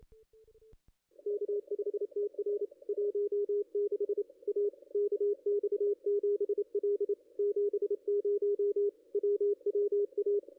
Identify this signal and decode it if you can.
Morse code, heard as a keyed tone, about 28 wpm, 410 Hz, with radio noise. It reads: K5TR 16 A KX7L 70 WWA
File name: contests / ss-cw-k5tr-op-n5rz -2200